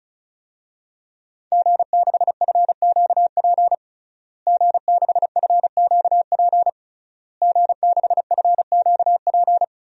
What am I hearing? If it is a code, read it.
Morse code, heard as a keyed tone, about 36 wpm, 700 Hz. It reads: G6FQP G6FQP G6FQP
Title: DXCC-100 Call Signs - 5 Characters - Repeated 3x - 35wpm